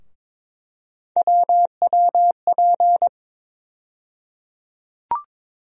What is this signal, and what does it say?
Morse code, heard as a keyed tone, about 21 wpm, 700 Hz. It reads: WWP
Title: USA Call Sign Postfixes - 3 Letters 22wpm